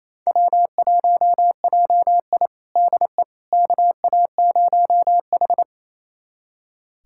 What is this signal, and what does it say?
Morse code, heard as a keyed tone, about 28 wpm, 700 Hz. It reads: W1JI DE KA0H
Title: USA Callsign DE Callsign - 4 Characters - 28wpm